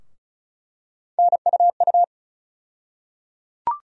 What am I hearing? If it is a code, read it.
Morse code, heard as a keyed tone, about 35 wpm, 700 Hz. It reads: NUU
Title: USA Call Sign Postfixes - 3 Letters 35wpm